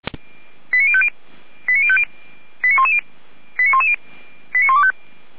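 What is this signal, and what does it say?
Five-tone selective call (ZVEI), heard as quick runs of five tones, tones of about 70 ms. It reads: 89050 89110 89110 89125
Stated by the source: ZVEI five-tone call received by SDR, archived on the Feuerwehr Orlamünde audio server (f85360).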